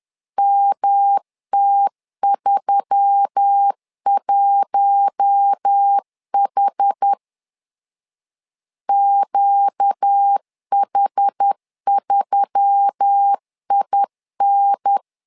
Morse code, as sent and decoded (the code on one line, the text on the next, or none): -- - ...-- .---- .... / --.- .... ...-- .. -.
MT31H QH3IN